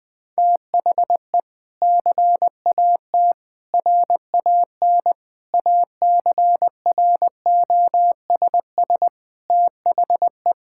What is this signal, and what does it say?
Morse code, heard as a keyed tone, about 20 wpm, 700 Hz. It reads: THE CAT RAN ACROSS THE
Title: Sentences with 2-Word Phrases - 20wpm